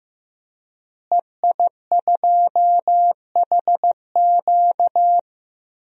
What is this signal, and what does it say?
Morse code, heard as a keyed tone, about 15 wpm, 700 Hz. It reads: EI2HQ
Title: DXCC-100 Call Signs - 5 Characters - Repeated 3x - 15wpm